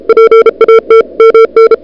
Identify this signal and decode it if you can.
Morse code, heard as a keyed tone, about 37 wpm, 440 Hz. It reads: PAT MN
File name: CWSprint / February 2007 CW Sprint / 01-2007 CW Sprint Hour1